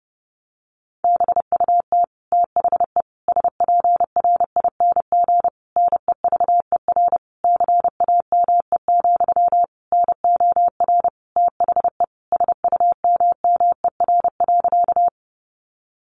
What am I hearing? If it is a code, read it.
Morse code, heard as a keyed tone, about 30 wpm, 700 Hz. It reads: BUT THE SPRING NEVER CAME, NOR THE SUMMER.
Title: Sentences from Top 1500 Words 30wpm